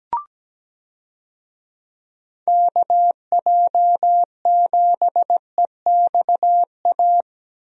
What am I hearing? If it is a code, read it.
Morse code, heard as a keyed tone, about 17 wpm, 700 Hz. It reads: KJ7EXA
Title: USA Call Signs - 6 Characters - Repeated 2x - 17wpm